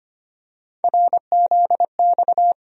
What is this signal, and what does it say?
Morse code, heard as a keyed tone, about 25 wpm, 700 Hz. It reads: RZX
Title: USA Call Sign Postfixes - 3 Letters 25wpm